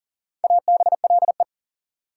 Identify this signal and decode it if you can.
Morse code, heard as a keyed tone, about 40 wpm, 700 Hz. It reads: ABLE
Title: Sentences from Top 700 Words - New Word Drill - 40wpm